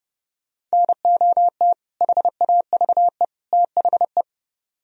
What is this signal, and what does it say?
Morse code, heard as a keyed tone, about 37 wpm, 690 Hz. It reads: NOT HAVE THE
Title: Speed-Racing - Sets of 3 Words 30wpm